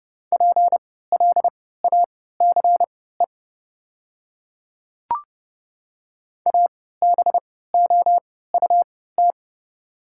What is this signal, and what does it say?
Morse code, heard as a keyed tone, about 29 wpm, spaced as for 17 wpm, 690 Hz. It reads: PLACE ABOUT
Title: 3x Character Spacing - Top 200 Words - 30wpm